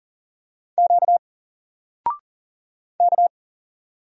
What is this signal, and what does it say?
Morse code, heard as a keyed tone, about 40 wpm, 700 Hz. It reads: Q K
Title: Single Letter-Number 40wpm